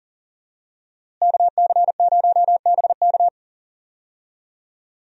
Morse code, heard as a keyed tone, about 40 wpm, 700 Hz. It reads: KC0BK
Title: USA Call Signs - 5 Characters - Repeated 3x - 40wpm